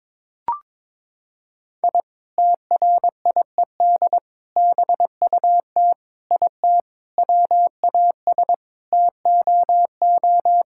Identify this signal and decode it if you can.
Morse code, heard as a keyed tone, about 22 wpm, 700 Hz. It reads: I TRIED BUT IT WAS TOO
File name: practice / 22wpm / Sentences with 3-Word Phrases - 22wpm